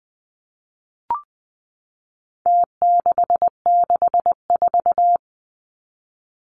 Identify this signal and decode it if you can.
Morse code, heard as a keyed tone, about 20 wpm, 700 Hz. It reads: T664
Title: Numbers - Contest Style - 4-Digit 20wpm